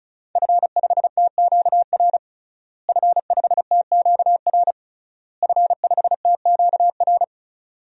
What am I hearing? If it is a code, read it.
Morse code, heard as a keyed tone, about 36 wpm, 700 Hz. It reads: F5TQR F5TQR F5TQR
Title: DXCC-100 Call Signs - 5 Characters - Repeated 3x - 35wpm